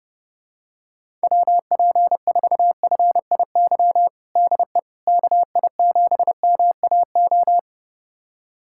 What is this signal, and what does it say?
Morse code, heard as a keyed tone, about 30 wpm, 700 Hz. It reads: WP4FIY DE KI7MAO
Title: USA Callsign DE Callsign 30wpm